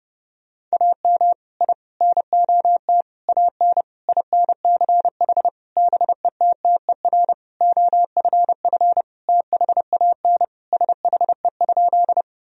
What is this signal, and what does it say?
Morse code, heard as a keyed tone, about 30 wpm, 700 Hz. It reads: AM I NOT AN INCH BETTER OFF THAN SHE?
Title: Sentences from Top 1000 Words - Review Part 3 - 30wpm